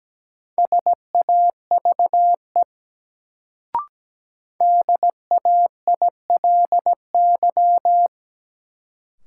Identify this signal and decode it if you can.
Morse code, heard as a keyed tone, about 17 wpm, 700 Hz. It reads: SAVE DAILY
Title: Sentences from Top 1100 Words - New Word Drill - 17wpm